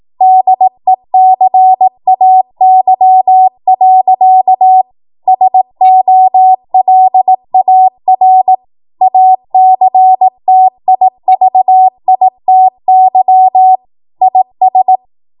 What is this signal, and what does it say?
Morse code, heard as a keyed tone, about 18 wpm, 750 Hz. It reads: DECAY. SOLAR ACTIVITY IS
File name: Morse / 18WPM